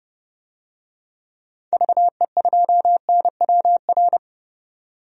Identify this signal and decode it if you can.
Morse code, heard as a keyed tone, about 30 wpm, 700 Hz. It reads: VE2NWR
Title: DXCC-100 Call Signs - 6 Characters - 30wpm